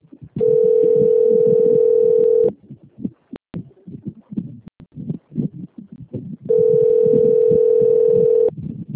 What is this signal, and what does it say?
Ringback tone (North America), 440 + 480 Hz.